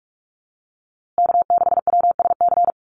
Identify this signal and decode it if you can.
Morse code, heard as a keyed tone, about 44 wpm, 690 Hz. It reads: K6WSC
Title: USA Call Signs Repeated 2x 45wpm